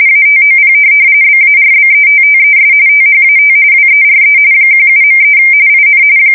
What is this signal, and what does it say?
RTTY, about 45.45 baud. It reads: ATOP BARBETTES THAT WERE 17.3 IN
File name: fsk170 / USS-Massachusetts